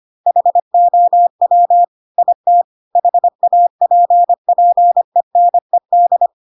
Morse code, heard as a keyed tone, about 25 wpm, 700 Hz. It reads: HOW IT HAPPENED
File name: Pinocchio / Pinocchio ch1 25 25 ik1hge